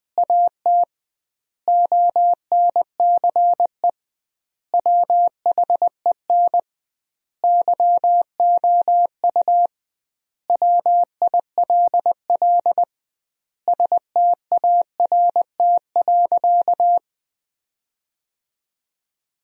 Morse code, heard as a keyed tone, about 21 wpm, 700 Hz. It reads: AT ONCE WHEN YOU WILL START.
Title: 2x Word-Spacing - Sentences from Top 300 Words - 20wpm